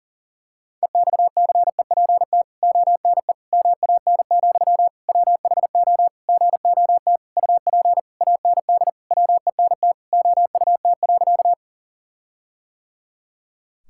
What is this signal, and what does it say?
Morse code, heard as a keyed tone, about 40 wpm, 700 Hz. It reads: EXCEPT ONE MAN, WHO GOT UP AND WENT OUT.
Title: Sentences from Top 1600 Words 40wpm